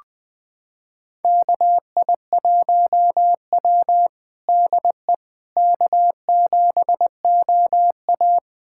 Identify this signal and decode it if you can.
Morse code, heard as a keyed tone, about 20 wpm, 700 Hz. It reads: KI1W DE K7OA
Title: USA Callsign DE Callsign - 4 Characters - 20wpm